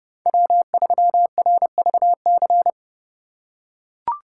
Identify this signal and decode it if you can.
Morse code, heard as a keyed tone, about 28 wpm, 700 Hz. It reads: W3RVC